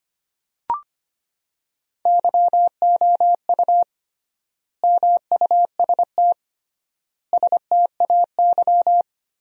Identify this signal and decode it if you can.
Morse code, heard as a keyed tone, about 25 wpm, 700 Hz. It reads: YOU MUST STAY
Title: 3x Word-Spacing - Sentences from Top 500 Words - 25wpm